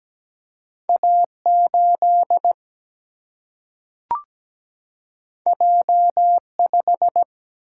Morse code, heard as a keyed tone, about 17 wpm, 700 Hz. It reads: A8 J5
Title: Single-Letter Single-Number 17wpm